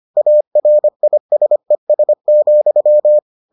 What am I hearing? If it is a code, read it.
Morse code, heard as a keyed tone, about 25 wpm, 600 Hz. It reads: ARISES,